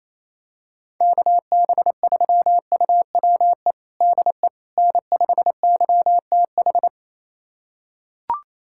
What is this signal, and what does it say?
Morse code, heard as a keyed tone, about 28 wpm, 700 Hz. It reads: KB3UWE DE N5YTH
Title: USA Callsign DE Callsign 28wpm